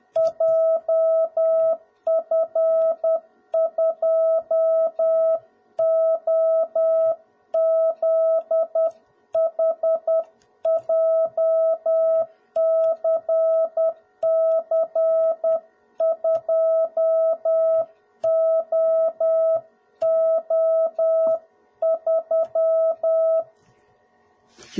Morse code, heard as a keyed tone, about 10 wpm, 650 Hz. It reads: JF2OZH JCC2OO3
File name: jh8cbh / 2021cw / cw01b